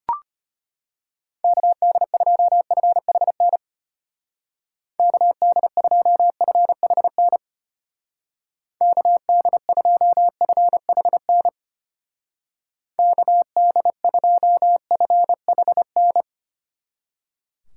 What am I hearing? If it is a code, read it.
Morse code, heard as a keyed tone, about 33 wpm, 700 Hz. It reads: KD2FHN KD2FHN KD2FHN KD2FHN